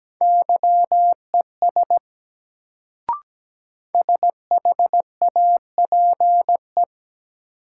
Morse code, heard as a keyed tone, about 17 wpm, 700 Hz. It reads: YES SHAPE
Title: Sentences from Top 500 Words - New Word Drill - 17wpm